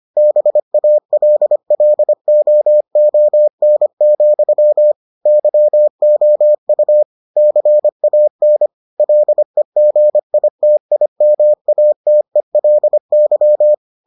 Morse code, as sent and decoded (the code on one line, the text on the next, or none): -... .- .-.. .-.. --- --- -. --..-- / -.-- --- ..- / -.-. .- -. / .-.. --. .. - .. -- .- - . .-.. -.--
BALLOON, YOU CAN LEGITIMATELY